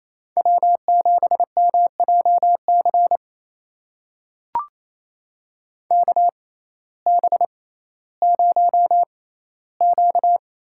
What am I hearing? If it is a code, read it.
Morse code, heard as a keyed tone, about 28 wpm, spaced as for 15 wpm, 700 Hz. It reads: W7MJC KB0Q
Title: USA Call Signs - 6x Character Spacing - 28wpm